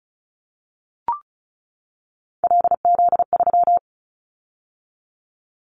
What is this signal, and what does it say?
Morse code, heard as a keyed tone, about 36 wpm, 700 Hz. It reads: LZ3
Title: Callsign Prefixes for Top 50 DXCC - 35wpm